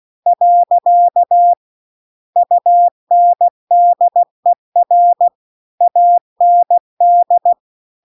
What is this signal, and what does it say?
Morse code, heard as a keyed tone, about 16 wpm, 700 Hz. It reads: . UNDER AND